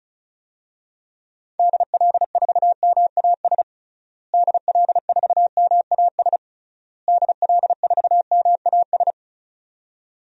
Morse code, heard as a keyed tone, about 36 wpm, 700 Hz. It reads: DL4MAS DL4MAS DL4MAS
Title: DXCC-100 Call Signs - 6 Characters - Repeated 3x - 35wpm